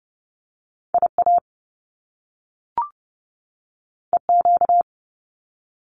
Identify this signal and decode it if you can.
Morse code, heard as a keyed tone, about 30 wpm, 700 Hz. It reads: IA EQ
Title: Top Two Letter Combinations in Words 30wpm